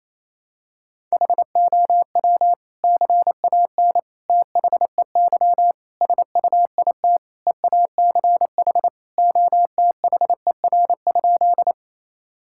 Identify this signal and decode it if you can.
Morse code, heard as a keyed tone, about 28 wpm, 700 Hz. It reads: HOW CAN THEY SUIT EACH OTHER?
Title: Sentences from Top 1600 Words 28wpm